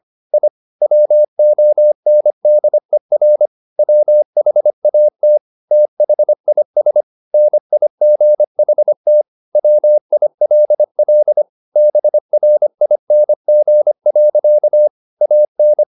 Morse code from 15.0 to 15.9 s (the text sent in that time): AN